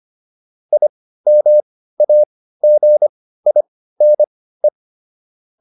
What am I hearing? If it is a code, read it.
Morse code, heard as a keyed tone, about 25 wpm, spaced as for 15 wpm, 600 Hz. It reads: IMAGINE